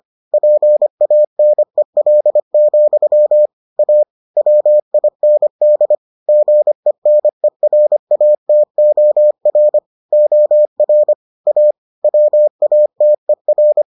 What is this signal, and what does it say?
Morse code, heard as a keyed tone, about 25 wpm, 600 Hz. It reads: PANEL, A WIND GENERATOR OR A WATER